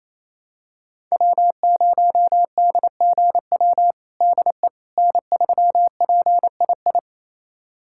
Morse code, heard as a keyed tone, about 28 wpm, 700 Hz. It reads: W0DGW DE N3PII
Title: USA Callsign DE Callsign - 5 Characters - 28wpm